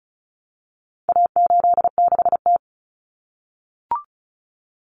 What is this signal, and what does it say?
Morse code, heard as a keyed tone, about 34 wpm, 700 Hz. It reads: A86T
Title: Common Cut Numbers - 4-Digit 35wpm